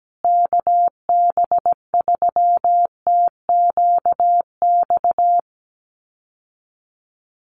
Morse code, heard as a keyed tone, about 17 wpm, 700 Hz. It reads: KB3TQX
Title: USA Call Signs Repeated 2x 17wpm